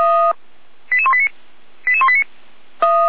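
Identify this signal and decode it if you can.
Five-tone selective call (ZVEI), heard as quick runs of five tones, tones of about 70 ms. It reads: 88189 88189
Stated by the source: ZVEI five-tone selective call 88166, received on SDR.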